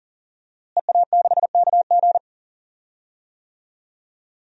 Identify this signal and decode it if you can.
Morse code, heard as a keyed tone, about 40 wpm, 700 Hz. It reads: EA6KG
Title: DXCC-100 Call Signs - 5 Characters - Repeated 2x - 40wpm